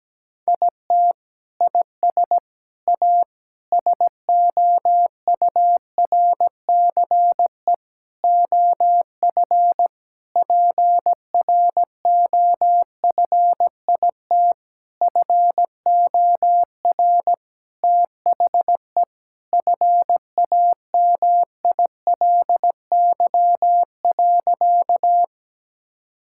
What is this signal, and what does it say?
Morse code, heard as a keyed tone, about 17 wpm, 700 Hz. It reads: IT IS A SOURCE OF PROFIT FOR THE FAMILY.
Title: Sentences from Top 1000 Words - Review Part 5 - 17wpm